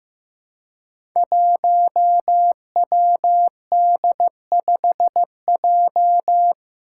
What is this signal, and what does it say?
Morse code, heard as a keyed tone, about 15 wpm, 700 Hz. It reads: 1WD5J
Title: Five Challenging Letter-Number Combinations 15wpm